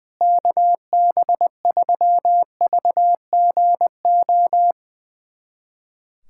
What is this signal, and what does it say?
Morse code, heard as a keyed tone, about 20 wpm, 700 Hz. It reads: KB3VGO
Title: USA Callsign DE Callsign - 6 Characters - 20wpm